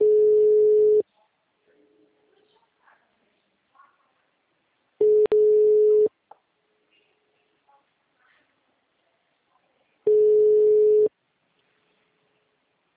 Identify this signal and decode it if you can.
Ringback tone (Europe, 425 Hz), with speech.